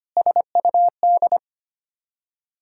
Morse code, heard as a keyed tone, about 25 wpm, 690 Hz. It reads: SUD
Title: Ontario QSO Party Exchange 25wpm